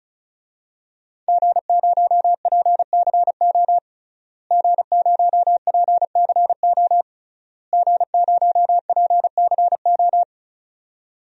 Morse code, heard as a keyed tone, about 36 wpm, 700 Hz. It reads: G0PCO G0PCO G0PCO
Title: DXCC-100 Call Signs - 5 Characters - Repeated 3x - 35wpm